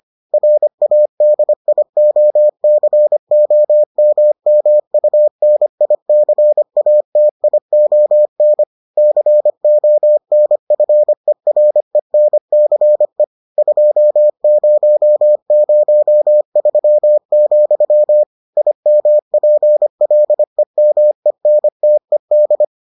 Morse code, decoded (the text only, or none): RADIOCOMMUNICATION CONFERENCE 2003, IMPLEMENTED